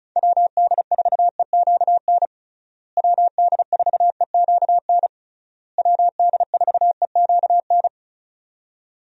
Morse code, heard as a keyed tone, about 36 wpm, 700 Hz. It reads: WD4EQN WD4EQN WD4EQN
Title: USA Call Signs - 6 Characters - Repeated 3x - 35wpm